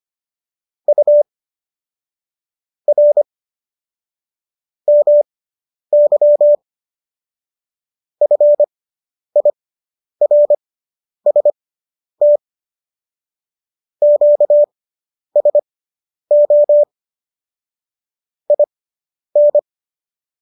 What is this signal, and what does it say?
Morse code, heard as a keyed tone, about 25 wpm, spaced as for 10 wpm, 590 Hz. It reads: U R MY FIRST QSO IN